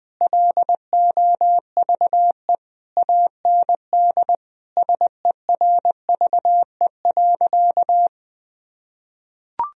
Morse code, heard as a keyed tone, about 20 wpm, 700 Hz. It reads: LOVE AND SERVE.
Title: Sentences from Top 1000 Words - Review Part 2 - 20wpm